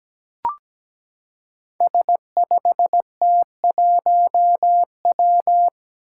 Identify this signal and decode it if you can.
Morse code, heard as a keyed tone, about 17 wpm, 700 Hz. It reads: S5T1W
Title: Five Challenging Letter-Number Combinations 17wpm